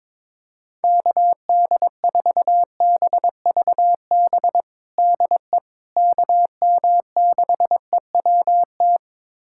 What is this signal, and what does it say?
Morse code, heard as a keyed tone, about 21 wpm, 700 Hz. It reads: KD4BVB DE KM6EWT
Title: USA Callsign DE Callsign - 6 Characters - 22wpm